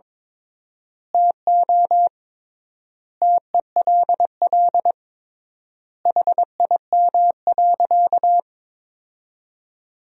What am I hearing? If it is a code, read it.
Morse code, heard as a keyed tone, about 22 wpm, 700 Hz. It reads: TO TELL HIM.
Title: 3x Word-Spacing - Sentences from Top 200 Words - 22wpm